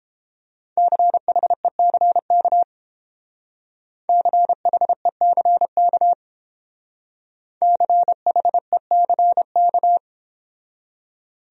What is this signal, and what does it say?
Morse code, heard as a keyed tone, about 28 wpm, 700 Hz. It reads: CHECK CHECK CHECK